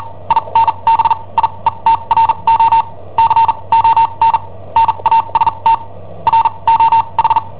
Morse code, heard as a keyed tone, about 43 wpm, spaced as for 28 wpm, 910 Hz. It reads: INBIETRO CON NASTRO H
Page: Cassetta esercizi preparata da I7OHP per IW5ELP